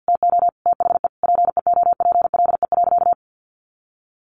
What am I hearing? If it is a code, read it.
Morse code, heard as a keyed tone, about 50 wpm, 700 Hz. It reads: TO THE PEOPLE.
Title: Sentences from Top 400 Words 50wpm